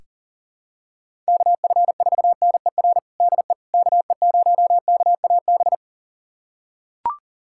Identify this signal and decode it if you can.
Morse code, heard as a keyed tone, about 37 wpm, 700 Hz. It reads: KF4NER DE KE0KAB
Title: USA Callsign DE Callsign 40wpm